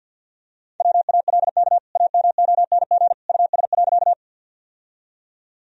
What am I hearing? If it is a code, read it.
Morse code, heard as a keyed tone, about 47 wpm, 700 Hz. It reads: WALK AMONG US.
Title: Sentences from Top 1000 Words - Review Part 3 - 50wpm